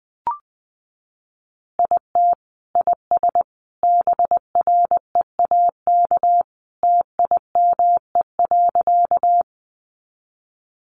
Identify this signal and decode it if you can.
Morse code, heard as a keyed tone, about 20 wpm, 700 Hz. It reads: IT IS BREAK TIME.
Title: Easy Phrases 20wpm